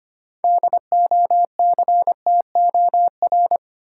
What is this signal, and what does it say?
Morse code, heard as a keyed tone, about 25 wpm, 700 Hz. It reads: DOCTOR
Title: Six Letter Words - 25wpm